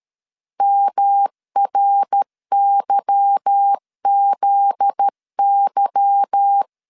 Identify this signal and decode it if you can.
Morse code, heard as a keyed tone, about 13 wpm, 800 Hz. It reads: MRYZY
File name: LEZIONI / 8 YJ45 / 8 Y 12wpm 2228